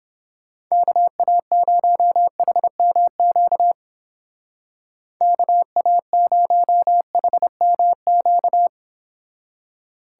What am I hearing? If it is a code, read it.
Morse code, heard as a keyed tone, about 29 wpm, 700 Hz. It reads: KA0HMQ KA0HMQ